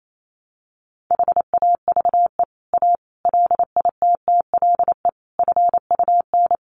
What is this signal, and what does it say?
Morse code, heard as a keyed tone, about 28 wpm, 700 Hz. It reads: HAVE A LITTLE FUN
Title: Sentences from Top 700 Words 28wpm